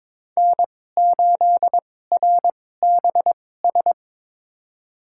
Morse code, heard as a keyed tone, about 21 wpm, spaced as for 16 wpm, 700 Hz. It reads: N8RBS
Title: USA Call Signs - 5 Characters - 2x Character Spacing - 22wpm